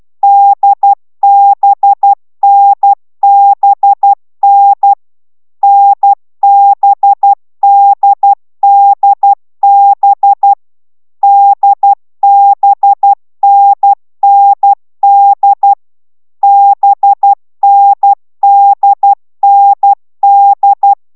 Morse code, heard as a keyed tone, about 12 wpm, 800 Hz. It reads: DBNBN NBDDB DBNND BNDND